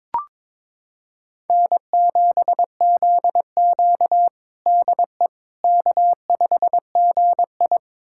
Morse code, heard as a keyed tone, about 22 wpm, 700 Hz. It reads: N7ZQ DE K5GI
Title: USA Callsign DE Callsign - 4 Characters - 22wpm